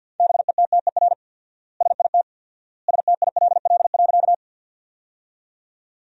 Morse code, heard as a keyed tone, about 52 wpm, 700 Hz. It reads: BETTER SIT STILL.
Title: 4x Word-Spacing - Sentences from Top 500 Words - 50wpm